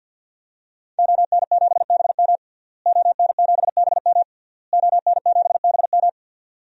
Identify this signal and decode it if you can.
Morse code, heard as a keyed tone, about 50 wpm, 700 Hz. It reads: ON7BM ON7BM ON7BM
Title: DXCC-100 Call Signs - 5 Characters - Repeated 3x - 50wpm